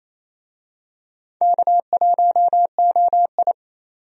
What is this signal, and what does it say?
Morse code, heard as a keyed tone, about 27 wpm, 700 Hz. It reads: K1OI
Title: USA Call Signs - 4 Characters - 28wpm